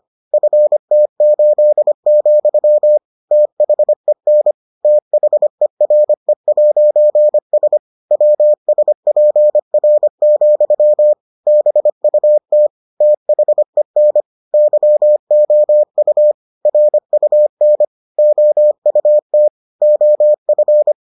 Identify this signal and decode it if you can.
Morse code, heard as a keyed tone, about 25 wpm, 600 Hz. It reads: FT8, THEN THERE'S WSPR, BUT THEN YOU RUN OUT OF